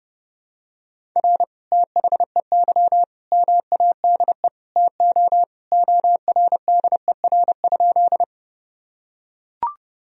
Morse code, heard as a keyed tone, about 32 wpm, 700 Hz. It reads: R THEY MADE TO ORDER?
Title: Sentences from Top 400 Words and Abbreviations 30wpm